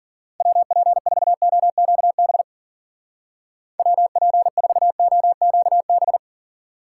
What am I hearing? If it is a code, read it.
Morse code, heard as a keyed tone, about 47 wpm, 700 Hz. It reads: WP4OQB WP4OQB
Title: Speed Racing - USA 2x3 Call Signs 40wpm